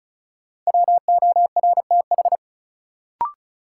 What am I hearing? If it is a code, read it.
Morse code, heard as a keyed tone, about 34 wpm, 700 Hz. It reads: WORTH
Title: Sentences from Top 800 Words - New Word Drill - 35wpm